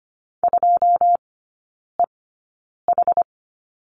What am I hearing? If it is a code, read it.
Morse code, heard as a keyed tone, about 25 wpm, spaced as for 9 wpm, 700 Hz. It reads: 2EH
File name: course / 25:10wpm / 093 Lesson 16e - Letter c - Three Characters 25wpm